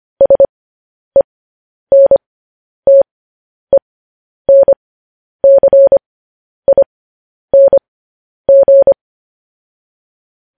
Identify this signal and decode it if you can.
Morse code, heard as a keyed tone, about 25 wpm, spaced as for 10 wpm, 550 Hz. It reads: SENTENCING
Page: news-headlines-in-morse-code-at-10-wpm